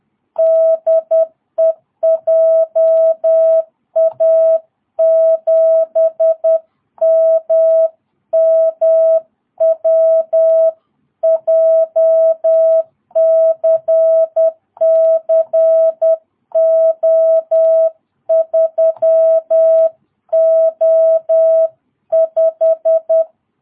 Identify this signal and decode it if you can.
Morse code, heard as a keyed tone, about 10 wpm, 660 Hz. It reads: DE JA7MMW JCCO3O5
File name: jh8cbh / 2022cw